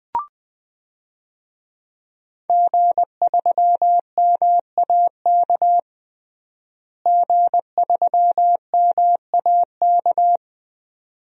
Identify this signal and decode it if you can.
Morse code, heard as a keyed tone, about 20 wpm, 700 Hz. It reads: G3MAK G3MAK
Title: DXCC-100 Call Signs - 5 Characters - Repeated 2x - 20wpm